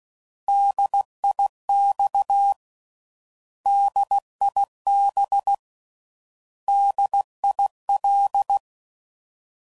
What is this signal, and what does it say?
Morse code, heard as a keyed tone, about 16 wpm, 800 Hz. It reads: DIX DIB DIL
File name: Prefixes / By First Letter / Ds